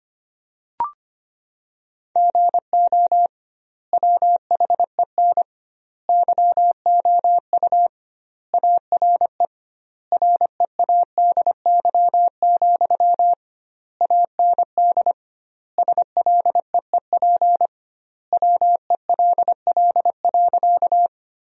Morse code, heard as a keyed tone, about 25 wpm, 700 Hz. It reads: GO WHEN YOU ARE READY, AND SLEEP WELL.
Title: 2x Word-Spacing - Sentences from Top 500 Words - 25wpm